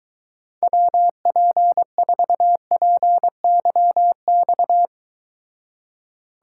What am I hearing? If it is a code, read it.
Morse code, heard as a keyed tone, about 22 wpm, 700 Hz. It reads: WP4PYX